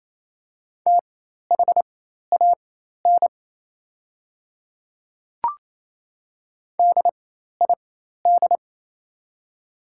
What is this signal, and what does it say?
Morse code, heard as a keyed tone, about 28 wpm, spaced as for 13 wpm, 690 Hz. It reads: THAN DID
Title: 4x Character Spacing - Top 300 Words - 28wpm